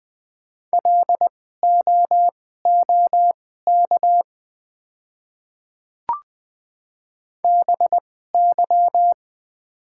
Morse code, heard as a keyed tone, about 20 wpm, spaced as for 14 wpm, 700 Hz. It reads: LOOK BY
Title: 2x Character Spacing - Top 100 Words - 20wpm